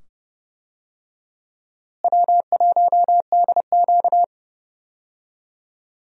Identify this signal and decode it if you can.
Morse code, heard as a keyed tone, about 30 wpm, 700 Hz. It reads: W1DQ
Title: USA Call Signs - 4 Characters - Repeated 2x - 30wpm